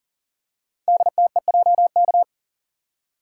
Morse code, heard as a keyed tone, about 42 wpm, 690 Hz. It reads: DTEJK
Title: Code Groups - 5 Letters - 40wpm